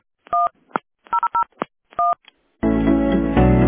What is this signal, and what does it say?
DTMF, heard as telephone touch tones, tones of about 170 ms. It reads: 1001
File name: ac7ed2ca-d59d-4c26-8f58